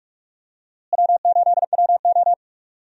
Morse code, heard as a keyed tone, about 42 wpm, 700 Hz. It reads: W8WO
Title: USA Call Signs - 4 Characters - Repeated 2x - 45wpm